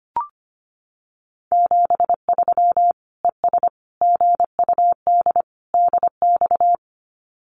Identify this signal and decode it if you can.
Morse code, heard as a keyed tone, about 25 wpm, 700 Hz. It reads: 73 ES GUD DX
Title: Common QSO Phrases 25wpm